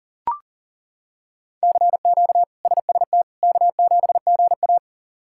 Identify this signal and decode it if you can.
Morse code, heard as a keyed tone, about 37 wpm, 700 Hz. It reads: CQ SST K7GA
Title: K1USN SST QSO Practice - 40wpm